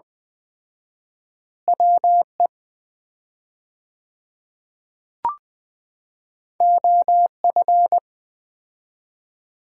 Morse code, heard as a keyed tone, about 20 wpm, 700 Hz. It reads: WE OF
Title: 4x Word-Spacing - Sets of 2 Words - 20wpm